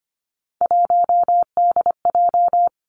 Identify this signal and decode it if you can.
Morse code, heard as a keyed tone, about 25 wpm, 700 Hz. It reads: 1DJ
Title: Three Challenging Letter-Number Combinations 25wpm